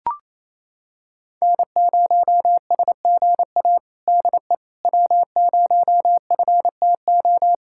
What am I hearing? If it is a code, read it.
Morse code, heard as a keyed tone, about 28 wpm, 700 Hz. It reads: N0SGA DE W0FTO
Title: USA Callsign DE Callsign - 5 Characters - 28wpm